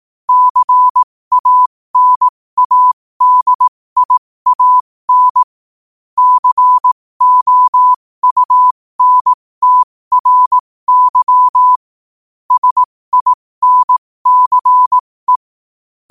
Morse code, heard as a keyed tone, about 18 wpm, spaced as for 15 wpm, 1000 Hz. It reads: CANADIAN COUNTRY SINCE
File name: qotdfiles / 15wpm